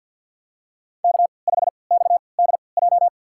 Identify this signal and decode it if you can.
Morse code, heard as a keyed tone, about 53 wpm, spaced as for 29 wpm, 700 Hz. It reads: K5XDJ
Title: USA Call Signs - 5 Characters - 3x Character Spacing - 50wpm